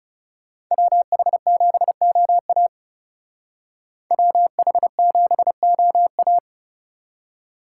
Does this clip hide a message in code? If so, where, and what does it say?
Yes: Morse code from 0.7 to 6.4 s, heard as a keyed tone, about 38 wpm, 700 Hz. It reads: WH7OA WH7OA